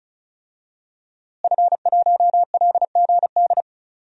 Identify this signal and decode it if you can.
Morse code, heard as a keyed tone, about 34 wpm, 700 Hz. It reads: F1LGD
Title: DXCC-100 Call Signs - 5 Characters - Repeated 3x - 35wpm